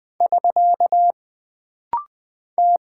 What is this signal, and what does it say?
Morse code, heard as a keyed tone, about 20 wpm, 700 Hz. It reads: <SK> T
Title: Single Character (Letter:Number:Punctuation:Prosign) - Rapid-Fire - 20wpm